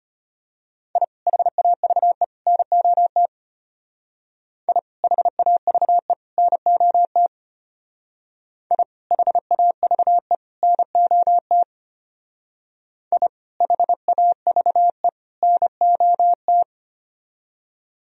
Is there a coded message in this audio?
Yes: Morse code from 1.0 to 16.6 s, heard as a keyed tone, about 33 wpm, 700 Hz. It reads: I HAVE NOT I HAVE NOT I HAVE NOT I HAVE NOT